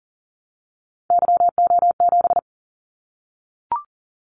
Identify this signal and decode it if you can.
Morse code, heard as a keyed tone, about 40 wpm, 700 Hz. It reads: YO7
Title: Callsign Prefixes for Top 25 DXCC - 40wpm